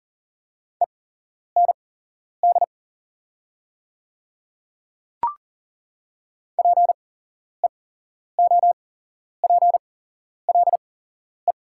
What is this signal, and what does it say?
Morse code, heard as a keyed tone, about 41 wpm, spaced as for 11 wpm, 700 Hz. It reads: END PEOPLE
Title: 8x Character Spacing - Top 300 Words - 40wpm